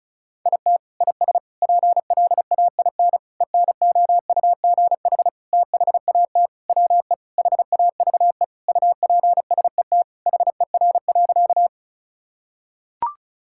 Morse code, heard as a keyed tone, about 36 wpm, 700 Hz. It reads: IT IS PLAIN ENOUGH THAT WE HAVE UPSET HER.
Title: Sentences from Top 1000 Words - Review Part 5 - 35wpm